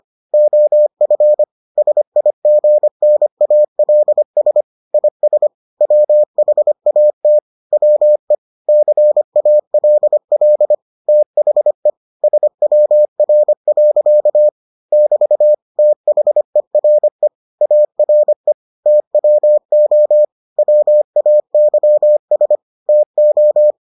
Morse code, heard as a keyed tone, about 25 wpm, 600 Hz. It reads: OF SIGNALS IS WHAT WE CALL THE SWR. <BT> THERE ARE TWO WAYS TO